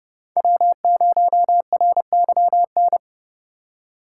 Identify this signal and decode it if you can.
Morse code, heard as a keyed tone, about 31 wpm, 700 Hz. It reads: W0RYN
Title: USA Call Signs - 5 Characters - 8x Character Spacing - 30wpm